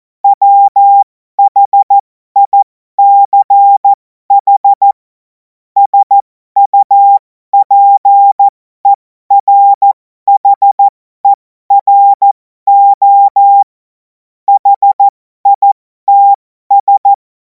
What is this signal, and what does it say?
Morse code, heard as a keyed tone, about 14 wpm, spaced as for 12 wpm, 800 Hz. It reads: WHICH SUPERHERO HITS
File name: Morsekurs / Witze